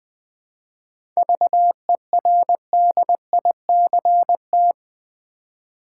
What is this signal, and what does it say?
Morse code, heard as a keyed tone, about 20 wpm, 700 Hz. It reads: VERDICT
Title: 7 Letter Words - First 3 Letters Given - 20wpm